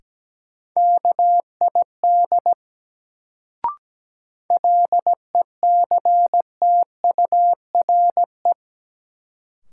Morse code, heard as a keyed tone, about 17 wpm, 700 Hz. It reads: KID LECTURE
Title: Sentences from Top 1600 Words - New Word Drill Alt - 17wpm